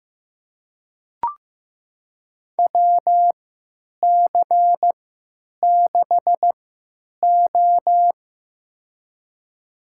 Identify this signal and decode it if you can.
Morse code, heard as a keyed tone, about 16 wpm, spaced as for 9 wpm, 700 Hz. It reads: WC6O
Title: USA Call Signs - 3x Character Spacing - 15wpm